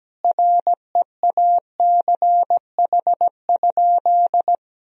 Morse code, heard as a keyed tone, about 17 wpm, 700 Hz. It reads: REACH?
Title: Sentences from Top 1000 Words - Review Part 4 - 17wpm